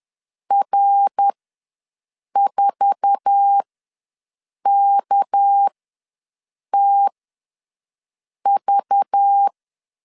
Morse code, heard as a keyed tone, about 10 wpm, spaced as for 6 wpm, 800 Hz. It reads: R4KTV
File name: LEZIONI / 10 SIGNS / 10A SLASH ws 10wpm 4121